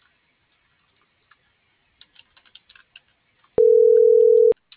Ringback tone (North America), 440 + 480 Hz.